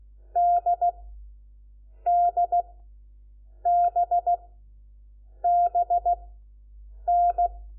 Morse code, heard as a keyed tone, about 16 wpm, spaced as for 7 wpm, 700 Hz. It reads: DDBBN
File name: Lecciones / CLASE Nº4